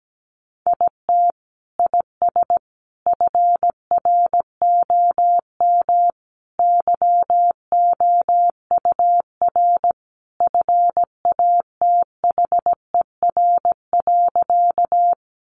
Morse code, heard as a keyed tone, about 17 wpm, 700 Hz. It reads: IT IS FROM YOUR FATHER.
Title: Sentences from Top 300 Words 17wpm